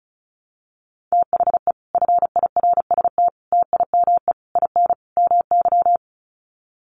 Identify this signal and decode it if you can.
Morse code, heard as a keyed tone, about 34 wpm, 700 Hz. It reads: THE FIRST TIME IN MY
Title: Sets of 5 Words Encore 35wpm